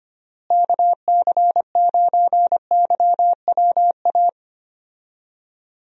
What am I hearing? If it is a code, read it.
Morse code, heard as a keyed tone, about 25 wpm, 700 Hz. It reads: KC9YWA
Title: USA Call Signs - 6 Characters - Repeated 2x - 25wpm